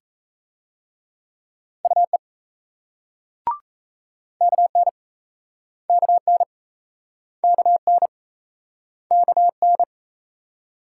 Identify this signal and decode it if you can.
Morse code, heard as a keyed tone, about 40 wpm, 700 Hz. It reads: UE KN KN KN KN